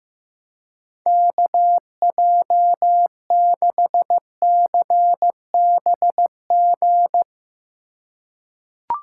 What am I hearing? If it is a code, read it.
Morse code, heard as a keyed tone, about 15 wpm, 700 Hz. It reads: KJ6CBG
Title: USA Call Signs - 6 Characters - 2x Character Spacing - 15wpm